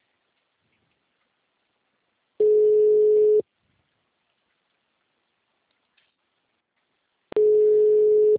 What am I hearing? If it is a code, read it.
Ringback tone (Europe, 425 Hz).
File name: out- -2016- - - .33442